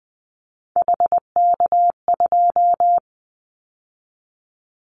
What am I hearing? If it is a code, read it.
Morse code, heard as a keyed tone, about 20 wpm, 700 Hz. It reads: HK2
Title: Callsign Prefixes for Top 100 DXCC - 20wpm